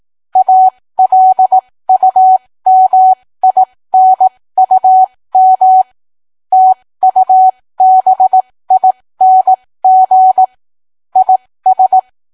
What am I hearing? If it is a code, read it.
Morse code, heard as a keyed tone, about 18 wpm, spaced as for 15 wpm, 750 Hz. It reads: ALUMINUM TUBING IS